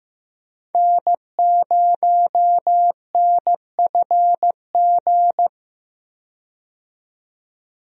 Morse code, heard as a keyed tone, about 15 wpm, 700 Hz. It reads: N0NFG
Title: USA Call Signs - 5 Characters - 15wpm